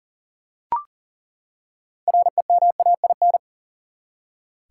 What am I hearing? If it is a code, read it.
Morse code, heard as a keyed tone, about 37 wpm, 700 Hz. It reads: REMAIN
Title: Six Letter Words - 40wpm